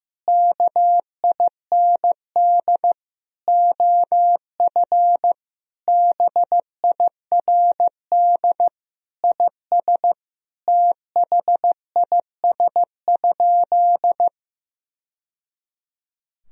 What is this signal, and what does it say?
Morse code, heard as a keyed tone, about 15 wpm, 700 Hz. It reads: KIND OF BIRD IS THIS?
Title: Sentences from Top 1000 Words - Review Part 2 - 15wpm